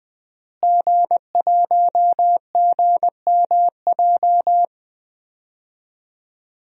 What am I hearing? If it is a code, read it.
Morse code, heard as a keyed tone, about 20 wpm, 700 Hz. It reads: G1GMJ
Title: DXCC-100 Call Signs - 5 Characters - Repeated 2x - 20wpm